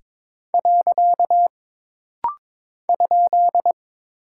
Morse code, heard as a keyed tone, about 22 wpm, 700 Hz. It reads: . ?
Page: This rapid-fire timing is geared to push your ICR (Instant Character Recognition) skills.